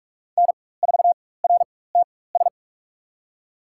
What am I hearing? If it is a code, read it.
Morse code, heard as a keyed tone, about 46 wpm, spaced as for 21 wpm, 700 Hz. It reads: N4RTS
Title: USA Call Signs - 5 Characters - 4x Character Spacing - 45wpm